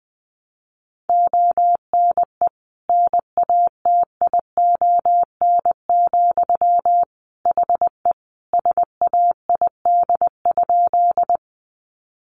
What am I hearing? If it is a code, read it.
Morse code, heard as a keyed tone, about 20 wpm, 700 Hz. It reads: ONE NATION, HE SAID?
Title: Sentences from Top 900 Words 20wpm